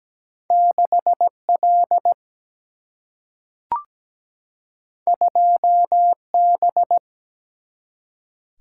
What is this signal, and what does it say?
Morse code, heard as a keyed tone, about 17 wpm, 700 Hz. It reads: 6L 2B
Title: Single-Letter Single-Number 17wpm